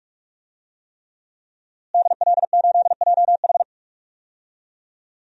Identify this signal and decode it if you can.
Morse code, heard as a keyed tone, about 44 wpm, 700 Hz. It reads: DL8JH
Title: DXCC-100 Call Signs - 5 Characters - Repeated 3x - 45wpm